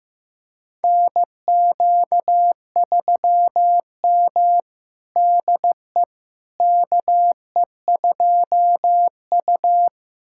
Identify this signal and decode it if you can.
Morse code, heard as a keyed tone, about 16 wpm, 700 Hz. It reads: NQ3M DE KE2U
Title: USA Callsign DE Callsign - 4 Characters - 15wpm